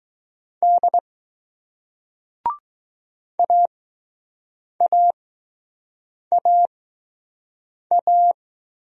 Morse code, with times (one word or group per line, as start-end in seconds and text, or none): 0.6-1.0 D
3.4-8.3 AAAA